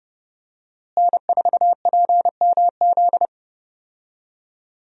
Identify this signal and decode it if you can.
Morse code, heard as a keyed tone, about 30 wpm, 700 Hz. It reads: N4PMZ
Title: USA Call Signs - 5 Characters - Repeated 3x - 30wpm